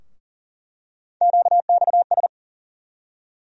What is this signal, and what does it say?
Morse code, heard as a keyed tone, about 40 wpm, 700 Hz. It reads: QXS
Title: USA Call Sign Postfixes - 3 Letters 40wpm